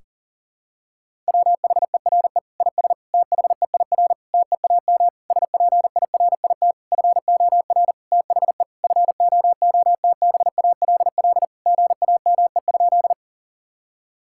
Morse code, heard as a keyed tone, about 40 wpm, 700 Hz. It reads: WHERE IS THEIR TEAM SPIRIT FOR THE FOOTBALL GAME?
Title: Sentences from Top 1600 Words 40wpm